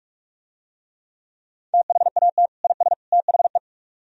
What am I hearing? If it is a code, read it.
Morse code, heard as a keyed tone, about 45 wpm, 700 Hz. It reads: THAT IS THE